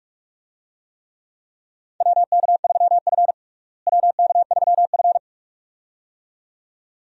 Morse code, heard as a keyed tone, about 46 wpm, 700 Hz. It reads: WK3F WK3F
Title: USA Call Signs - 4 Characters - Repeated 2x - 45wpm